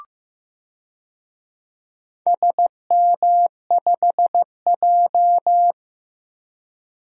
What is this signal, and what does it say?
Morse code, heard as a keyed tone, about 15 wpm, 700 Hz. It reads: SM5J
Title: DXCC-100 Call Signs - 4 Characters - Repeated 3x - 15wpm